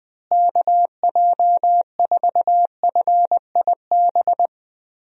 Morse code, heard as a keyed tone, about 21 wpm, 700 Hz. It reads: KJ4FIB
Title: USA Call Signs - 6 Characters - Repeated 3x - 20wpm